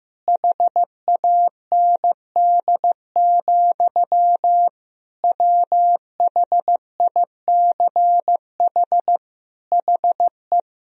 Morse code, heard as a keyed tone, about 15 wpm, 700 Hz. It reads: HAND, WHICH HE
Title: Sentences from Top 1000 Words - Review Part 3 - 15wpm